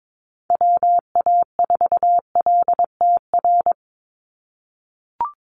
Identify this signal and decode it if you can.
Morse code, heard as a keyed tone, about 22 wpm, 700 Hz. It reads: WA4LTR
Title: USA Call Signs 22wpm